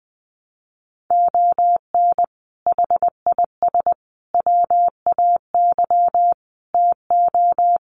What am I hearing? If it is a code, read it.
Morse code, heard as a keyed tone, about 20 wpm, 700 Hz. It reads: ON HIS WAY TO
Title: Sets of 5 Words Encore2 20wpm